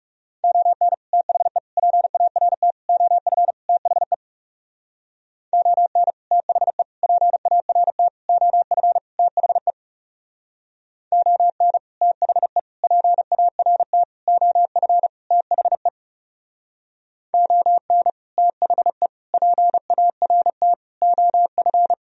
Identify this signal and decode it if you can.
Morse code, heard as a keyed tone, about 35 wpm, 700 Hz. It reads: ON THE PART OF THE ON THE PART OF THE ON THE PART OF THE ON THE PART OF
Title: Speed-Racing - Sets of 5 Words 30wpm